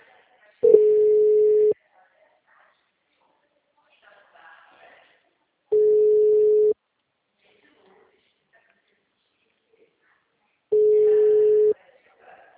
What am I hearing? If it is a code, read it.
Ringback tone (Europe, 425 Hz), with speech.